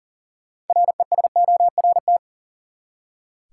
Morse code, heard as a keyed tone, about 40 wpm, 700 Hz. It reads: RESORT
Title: Six Letter Words - 40wpm